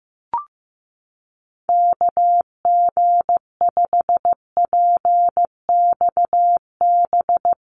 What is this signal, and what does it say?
Morse code, heard as a keyed tone, about 15 wpm, 700 Hz. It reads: KG5PXB
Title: USA Call Signs Repeated 2x 15wpm